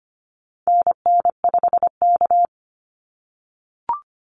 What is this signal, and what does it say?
Morse code, heard as a keyed tone, about 25 wpm, 690 Hz. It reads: NN5K
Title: NA SOTA Activator Callsigns 25wpm